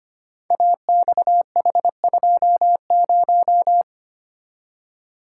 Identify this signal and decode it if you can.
Morse code, heard as a keyed tone, about 25 wpm, 700 Hz. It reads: AXH20
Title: Code Groups - 5 Letter:Numbers - 25wpm